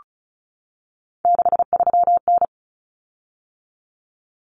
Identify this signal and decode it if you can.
Morse code, heard as a keyed tone, about 34 wpm, 700 Hz. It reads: 63N
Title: Common Cut Numbers - 3-Digit 35wpm